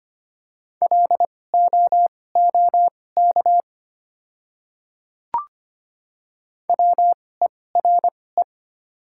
Morse code, heard as a keyed tone, about 25 wpm, spaced as for 18 wpm, 690 Hz. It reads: LOOK WERE
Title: 2x Character Spacing - Top 200 Words - 25wpm